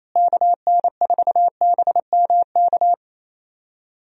Morse code, heard as a keyed tone, about 27 wpm, 700 Hz. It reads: KN4BMK
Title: USA Call Signs - 6 Characters - Repeated 3x - 28wpm